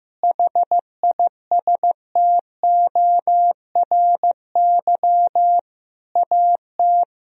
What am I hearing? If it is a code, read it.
Morse code, heard as a keyed tone, about 16 wpm, 700 Hz. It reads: HISTORY AT
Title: Sentences from Top 1000 Words - Review Part 4 - 15wpm